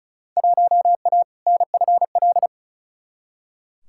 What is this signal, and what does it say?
Morse code, heard as a keyed tone, about 36 wpm, 700 Hz. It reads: 1A NFL
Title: Field Day Exchanges Encore - 35wpm